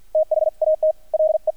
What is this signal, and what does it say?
Morse code, heard as a keyed tone, about 50 wpm, spaced as for 44 wpm, 630 Hz. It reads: THAT LE
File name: archived / CRSnet